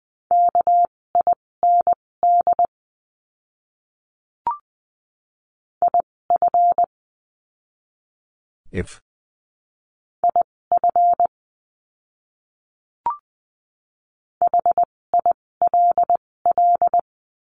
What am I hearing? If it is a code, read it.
Morse code, heard as a keyed tone, about 20 wpm, spaced as for 16 wpm, 700 Hz. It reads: KIND IF IF HILL
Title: 256 Lesson 42a - Words 12-20wpm